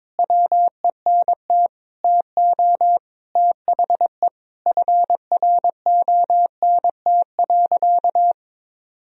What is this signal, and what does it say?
Morse code, heard as a keyed tone, about 21 wpm, 700 Hz. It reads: WENT TO THE FRONT.
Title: Sentences from Top 1000 Words - Review Part 2 - 22wpm